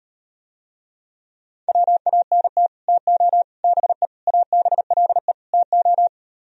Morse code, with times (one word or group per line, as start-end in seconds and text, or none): 1.7-2.7 WANT
2.9-3.4 TO
3.6-4.1 BE
4.3-5.3 ABLE
5.5-6.1 TO